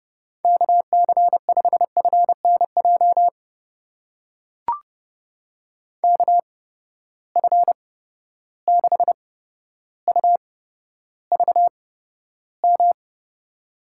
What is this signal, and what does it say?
Morse code, heard as a keyed tone, about 30 wpm, spaced as for 13 wpm, 700 Hz. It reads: KC5FNJ KF6UVM
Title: USA Call Signs - 6 Characters - 8x Character Spacing - 30wpm